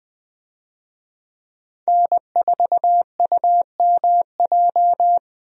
Morse code, heard as a keyed tone, about 21 wpm, 700 Hz. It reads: N4UMJ
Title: USA Call Signs - 5 Characters - Repeated 2x - 20wpm